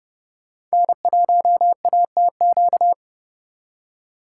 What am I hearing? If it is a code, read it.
Morse code, heard as a keyed tone, about 38 wpm, 700 Hz. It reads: N1ATQ